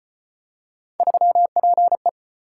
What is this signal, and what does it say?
Morse code, heard as a keyed tone, about 32 wpm, 700 Hz. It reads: 3PE